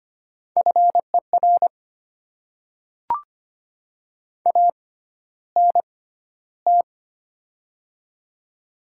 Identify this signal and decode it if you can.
Morse code, heard as a keyed tone, about 25 wpm, spaced as for 13 wpm, 700 Hz. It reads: FER ANT
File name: Top 20 QSO Elements - 6x Character Spacing - 25wpm